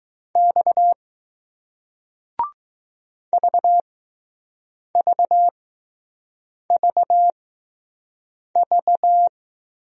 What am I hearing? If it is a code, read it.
Morse code, heard as a keyed tone, about 20 wpm, spaced as for 6 wpm, 700 Hz. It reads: X VVVV